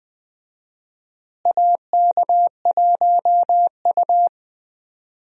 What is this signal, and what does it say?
Morse code, heard as a keyed tone, about 20 wpm, 700 Hz. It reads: AK1U
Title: USA Call Signs - 4 Characters - Repeated 2x - 20wpm